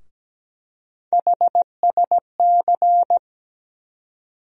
Morse code, heard as a keyed tone, about 17 wpm, 700 Hz. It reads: HSC
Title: USA Call Sign Postfixes - 3 Letters 17wpm